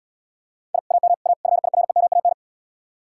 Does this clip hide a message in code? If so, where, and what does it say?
Yes: Morse code from 0.7 to 2.3 s, heard as a keyed tone, about 78 wpm, 700 Hz. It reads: I AM A FIREMAN